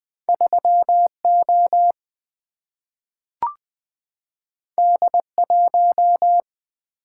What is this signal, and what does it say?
Morse code, heard as a keyed tone, about 20 wpm, 700 Hz. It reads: 3 O D 1
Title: Single-Letter Single-Number 20wpm